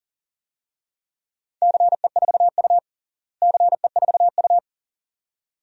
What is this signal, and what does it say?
Morse code, heard as a keyed tone, about 37 wpm, 700 Hz. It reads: CE4U CE4U
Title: DXCC-100 Call Signs - 4 Characters - Repeated 2x - 40wpm